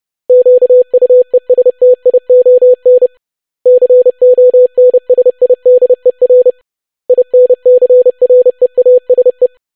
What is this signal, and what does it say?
Morse code, heard as a keyed tone, about 30 wpm, 500 Hz. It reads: QUESTION CONSIDER INCREASE
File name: basic500-25wpm